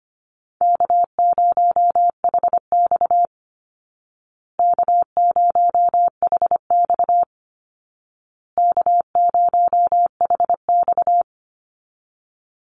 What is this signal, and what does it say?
Morse code, heard as a keyed tone, about 25 wpm, 700 Hz. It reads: K0HX K0HX K0HX
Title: Easy Call Signs Repeated 3x - 25wpm